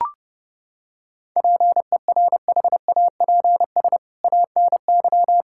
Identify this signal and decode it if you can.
Morse code, heard as a keyed tone, about 30 wpm, 700 Hz. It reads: PERHAPS ANY
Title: Sentences from Top 1600 Words 30wpm